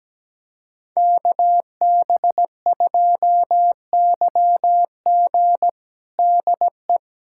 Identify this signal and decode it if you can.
Morse code, heard as a keyed tone, about 17 wpm, 700 Hz. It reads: KB2YG DE
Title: USA Callsign DE Callsign - 5 Characters - 17wpm